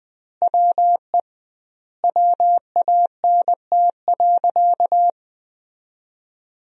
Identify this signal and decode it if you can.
Morse code, heard as a keyed tone, about 20 wpm, 700 Hz. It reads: WE WANT.
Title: 2x Word-Spacing - Sentences from Top 200 Words - 20wpm